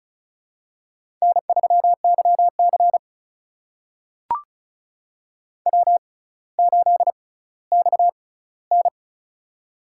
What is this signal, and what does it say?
Morse code, heard as a keyed tone, about 34 wpm, spaced as for 18 wpm, 700 Hz. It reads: N3YC W8XN
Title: USA Call Signs - 4 Characters - 6x Character Spacing - 35wpm